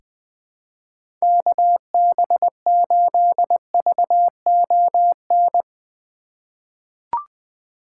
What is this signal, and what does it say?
Morse code, heard as a keyed tone, about 20 wpm, 700 Hz. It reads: KB8VON
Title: 2023 POTA Hunter Call Signs - 20wpm